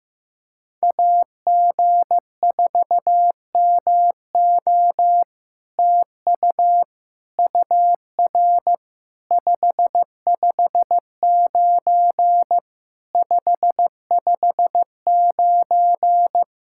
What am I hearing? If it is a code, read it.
Morse code, heard as a keyed tone, about 15 wpm, 700 Hz. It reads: AG4MO TU UR 559 559
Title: SOTA QSO Practice 15wpm